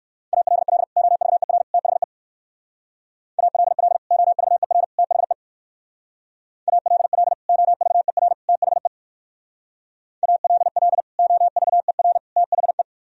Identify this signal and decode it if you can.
Morse code, heard as a keyed tone, about 56 wpm, 700 Hz. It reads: ALL OVER THE ALL OVER THE ALL OVER THE ALL OVER THE
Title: Speed-Racing - Sets of 3 Words 45wpm